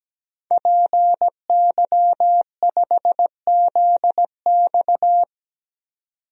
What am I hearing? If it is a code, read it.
Morse code, heard as a keyed tone, about 17 wpm, 700 Hz. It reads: PY5ZX
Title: DXCC-100 Call Signs - 5 Characters - 17wpm